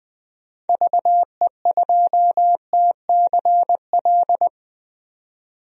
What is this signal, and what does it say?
Morse code, heard as a keyed tone, about 20 wpm, 700 Hz. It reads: VE2TCL
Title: DXCC-100 Call Signs - 6 Characters - 20wpm